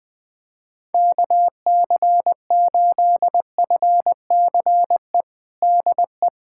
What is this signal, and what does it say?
Morse code, heard as a keyed tone, about 20 wpm, 700 Hz. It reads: KC8FCE DE
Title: USA Callsign DE Callsign - 6 Characters - 20wpm